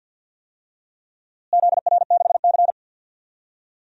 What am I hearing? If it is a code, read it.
Morse code, heard as a keyed tone, about 53 wpm, 700 Hz. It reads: ZR6C
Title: DXCC-100 Call Signs - 4 Characters - Repeated 3x - 50wpm